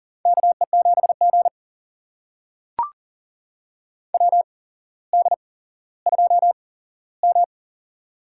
Morse code, heard as a keyed tone, about 40 wpm, spaced as for 17 wpm, 700 Hz. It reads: KE7G WD2M
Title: USA Call Signs - 4 Characters - 8x Character Spacing - 40wpm